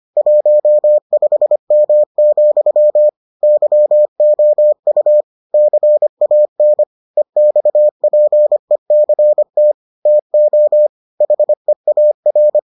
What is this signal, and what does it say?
Morse code, heard as a keyed tone, about 25 wpm, 600 Hz. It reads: 15M, YOU CAN EXPECT TO HEAR